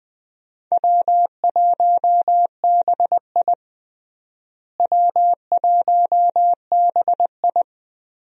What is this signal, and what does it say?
Morse code, heard as a keyed tone, about 20 wpm, 700 Hz. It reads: W1BI W1BI
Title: USA Call Signs - 4 Characters - Repeated 3x - 20wpm